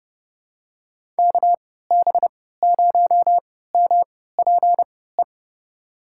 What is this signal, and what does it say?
Morse code, heard as a keyed tone, about 30 wpm, spaced as for 17 wpm, 700 Hz. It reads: KB0MPE
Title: USA Call Signs - 3x Character Spacing - 30wpm